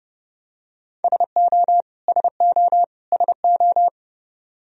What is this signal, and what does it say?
Morse code, heard as a keyed tone, about 32 wpm, 700 Hz. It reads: SO SO SO
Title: Top 100 Words Repeated 30wpm